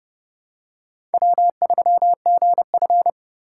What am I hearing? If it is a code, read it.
Morse code, heard as a keyed tone, about 32 wpm, 700 Hz. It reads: W3GF
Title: USA Call Signs - 4 Characters - 4x Character Spacing - 30wpm